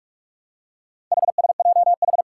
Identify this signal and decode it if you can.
Morse code, heard as a keyed tone, about 44 wpm, 700 Hz. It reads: HSJH
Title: Four Challenging Letter-Number Combinations 45wpm